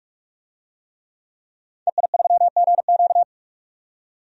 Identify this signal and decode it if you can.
Morse code, heard as a keyed tone, about 47 wpm, 700 Hz. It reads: EI3GQ
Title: DXCC-100 Call Signs - 5 Characters - 45wpm